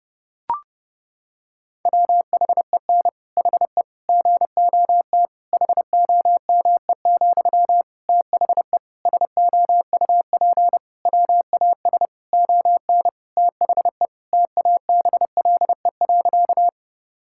Morse code, heard as a keyed tone, about 30 wpm, 700 Hz. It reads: WHEN HE GOT HOME, THE SOUP WAS ON THE TABLE.
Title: Sentences from Top 1000 Words - Review Part 6 - 30wpm